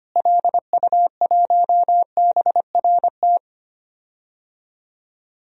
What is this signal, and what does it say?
Morse code, heard as a keyed tone, about 25 wpm, 700 Hz. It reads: LU1BRT
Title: DXCC-100 Call Signs - 6 Characters - Repeated 3x - 25wpm